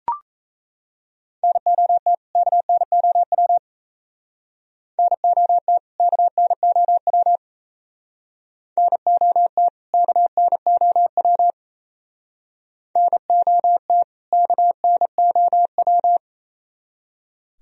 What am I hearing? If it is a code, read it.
Morse code, heard as a keyed tone, about 32 wpm, 700 Hz. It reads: NOT KNOW NOT KNOW NOT KNOW NOT KNOW
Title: Speed-Racing - Sets of 2 Words 28wpm